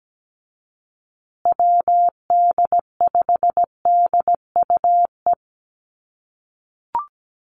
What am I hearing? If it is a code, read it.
Morse code, heard as a keyed tone, about 17 wpm, 700 Hz. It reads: WD5DUE
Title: USA Call Signs Repeated 3x 17wpm